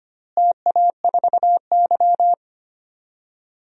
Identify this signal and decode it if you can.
Morse code, heard as a keyed tone, about 25 wpm, 700 Hz. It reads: TA4Y
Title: DXCC-100 Call Signs - 4 Characters - Repeated 3x - 25wpm